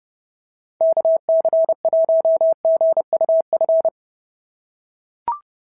Morse code, heard as a keyed tone, about 29 wpm, 650 Hz. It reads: KC1GUF
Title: SOTA QSO Practice 30wpm